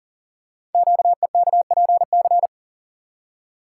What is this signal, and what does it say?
Morse code, heard as a keyed tone, about 38 wpm, 700 Hz. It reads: QEKPC
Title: Code Groups - 5 Letters - 40wpm